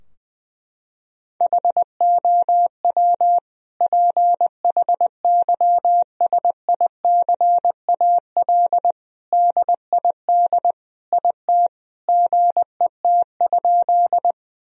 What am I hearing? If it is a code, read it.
Morse code, heard as a keyed tone, about 20 wpm, 700 Hz. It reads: HOW PHYSICAL DID IT GET?
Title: Sentences from Top 1000 Words - Review Part 4 - 20wpm